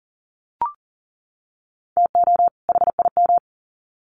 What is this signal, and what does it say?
Morse code, heard as a keyed tone, about 40 wpm, 700 Hz. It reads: TO HIM
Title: Sets of 2 Words 40wpm